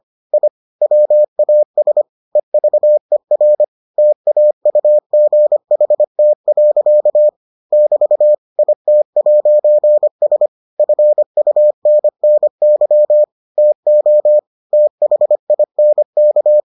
Morse code, heard as a keyed tone, about 25 wpm, 600 Hz. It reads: I WAS EVER TAUGHT. <BT> IT'S FUNNY TO THINK